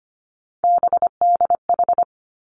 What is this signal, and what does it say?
Morse code, heard as a keyed tone, about 25 wpm, 700 Hz. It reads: BDH
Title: Three Challenging Letter-Number Combinations 25wpm